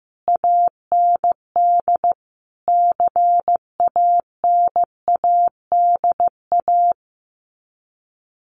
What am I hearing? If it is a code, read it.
Morse code, heard as a keyed tone, about 15 wpm, 700 Hz. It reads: AND CANADA
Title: Sets of 5 Words 15wpm